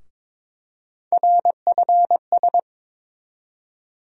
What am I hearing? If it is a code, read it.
Morse code, heard as a keyed tone, about 21 wpm, 700 Hz. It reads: RFS
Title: USA Call Sign Postfixes - 3 Letters 22wpm